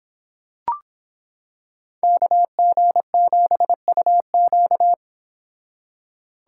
Morse code, heard as a keyed tone, about 26 wpm, 700 Hz. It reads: KG7UQ